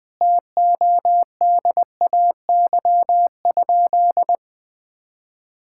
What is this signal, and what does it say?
Morse code, heard as a keyed tone, about 21 wpm, 700 Hz. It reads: TODAY?
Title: Sentences from Top 1600 Words 20wpm